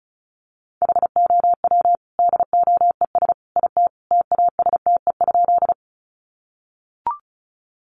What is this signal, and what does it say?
Morse code, heard as a keyed tone, about 34 wpm, 700 Hz. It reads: HOW DOES IT TASTE?
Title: Sentences from Top 1100 Words 35wpm